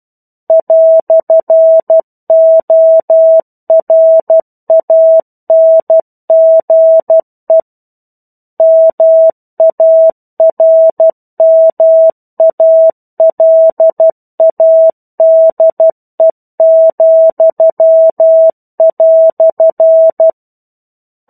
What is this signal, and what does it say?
Morse code, heard as a keyed tone, about 12 wpm, 650 Hz. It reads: "ORANGE MARMALADE,"
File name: 01-Chapter-01 12 WPM